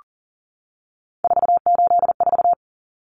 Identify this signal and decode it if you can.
Morse code, heard as a keyed tone, about 40 wpm, 700 Hz. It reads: 484
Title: Numbers - 3-Digit 40wpm